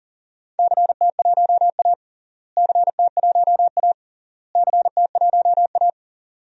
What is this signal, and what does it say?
Morse code, heard as a keyed tone, about 40 wpm, 700 Hz. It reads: CT1A CT1A CT1A
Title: DXCC-100 Call Signs - 4 Characters - Repeated 3x - 40wpm